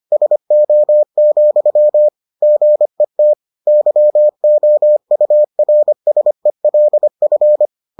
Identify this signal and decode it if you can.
Morse code, heard as a keyed tone, about 25 wpm, 600 Hz. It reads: SO, GET YOURSELF